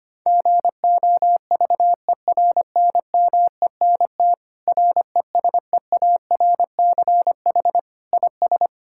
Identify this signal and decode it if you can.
Morse code, heard as a keyed tone, about 25 wpm, 700 Hz. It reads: GOVERNMENT RESEARCH IS
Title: Sentences from Top 1000 Words - Review Part 6 - 25wpm